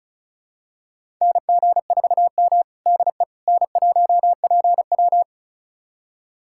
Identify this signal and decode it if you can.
Morse code, heard as a keyed tone, about 36 wpm, 700 Hz. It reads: NG4M DE N1PW
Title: USA Callsign DE Callsign - 4 Characters - 35wpm